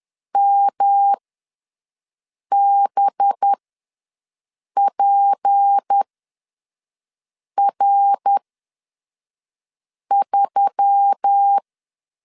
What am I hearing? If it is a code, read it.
Morse code, heard as a keyed tone, about 10 wpm, spaced as for 5 wpm, 800 Hz. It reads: MBPR3
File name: LEZIONI / 7 GF9Z / 7 GF ws 10wpm 3918